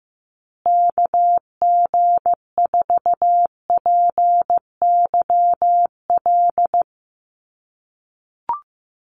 Morse code, heard as a keyed tone, about 14 wpm, 700 Hz. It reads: KG4PYL
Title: USA Call Signs Repeated 2x 15wpm